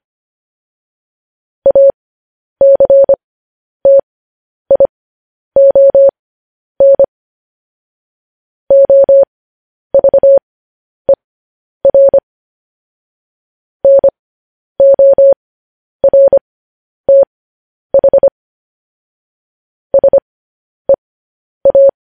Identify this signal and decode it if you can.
Morse code, heard as a keyed tone, about 25 wpm, spaced as for 10 wpm, 550 Hz. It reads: ACTION OVER NORTH SEA
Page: news headlines in morse code at 10 wpm